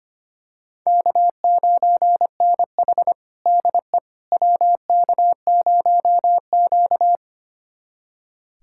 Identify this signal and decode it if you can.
Morse code, heard as a keyed tone, about 25 wpm, 700 Hz. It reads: K9NH DE WK0Q
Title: USA Callsign DE Callsign - 4 Characters - 25wpm